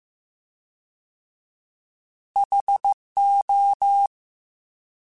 Morse code, heard as a keyed tone, about 15 wpm, 800 Hz. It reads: HO
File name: Alphabetical / Alphabetical mp3s / Hs @13